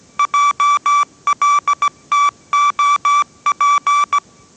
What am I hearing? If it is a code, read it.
Morse code, heard as a keyed tone, about 17 wpm, 1160 Hz. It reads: JLTOP
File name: 18wpm